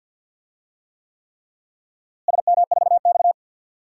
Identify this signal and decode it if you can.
Morse code, heard as a keyed tone, about 50 wpm, 700 Hz. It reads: SM4X
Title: DXCC-100 Call Signs - 4 Characters - Repeated 3x - 50wpm